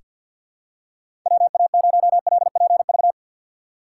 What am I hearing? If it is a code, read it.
Morse code, heard as a keyed tone, about 48 wpm, 700 Hz. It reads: WA0LPV
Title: K1USN SST QSO Practice - 50wpm